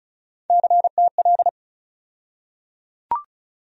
Morse code, heard as a keyed tone, about 36 wpm, 700 Hz. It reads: CTL
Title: USA Call Sign Postfixes - 3 Letters 35wpm